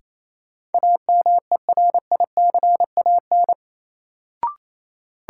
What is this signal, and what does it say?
Morse code, heard as a keyed tone, about 28 wpm, 700 Hz. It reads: AMERICAN
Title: Sets of 2 Words - New Word Drill - 28wpm